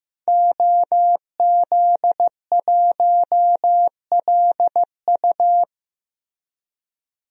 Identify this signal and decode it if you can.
Morse code, heard as a keyed tone, about 15 wpm, 700 Hz. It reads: OZ1LU
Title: DXCC-100 Call Signs - 5 Characters - Repeated 2x - 15wpm